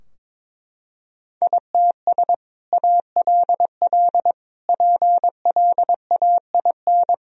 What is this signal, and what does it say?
Morse code, heard as a keyed tone, about 22 wpm, 700 Hz. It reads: ITS ALL PLAIN
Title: Sentences from Top 1000 Words - Review Part 2 - 22wpm